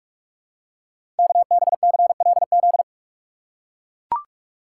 Morse code, heard as a keyed tone, about 44 wpm, 700 Hz. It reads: KBCLZ
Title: Code Groups - 5 Letters - 45wpm